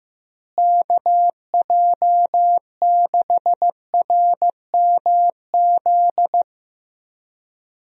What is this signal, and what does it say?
Morse code, heard as a keyed tone, about 15 wpm, 700 Hz. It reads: KJ6RMZ
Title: USA Call Signs - 6 Characters - Repeated 3x - 15wpm